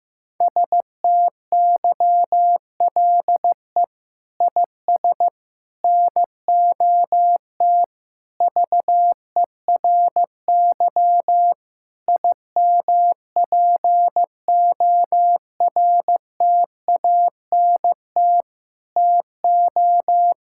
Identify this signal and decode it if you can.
Morse code, heard as a keyed tone, about 15 wpm, 700 Hz. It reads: STYLE IS NOT VERY IMPORTANT TO
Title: Sentences from Top 1000 Words - Review Part 5 - 15wpm